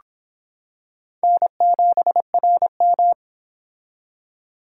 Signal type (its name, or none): Morse code